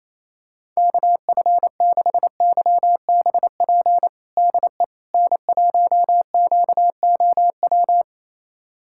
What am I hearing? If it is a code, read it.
Morse code, heard as a keyed tone, about 27 wpm, 700 Hz. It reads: KF6YBP DE N1QOW
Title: USA Callsign DE Callsign 28wpm